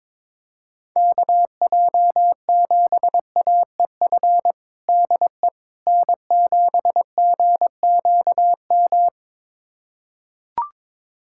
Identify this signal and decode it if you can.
Morse code, heard as a keyed tone, about 22 wpm, 700 Hz. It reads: KJ7AEF DE N7GQM
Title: USA Callsign DE Callsign 22wpm